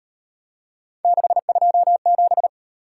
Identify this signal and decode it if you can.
Morse code, heard as a keyed tone, about 38 wpm, 700 Hz. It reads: 627